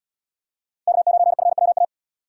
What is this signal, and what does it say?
Morse code, heard as a keyed tone, about 77 wpm, 700 Hz. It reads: K9FWA